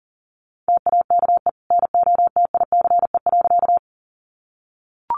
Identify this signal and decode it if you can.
Morse code, heard as a keyed tone, about 40 wpm, 700 Hz. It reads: TAKE NOTICE.
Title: Sentences from Top 400 Words 40wpm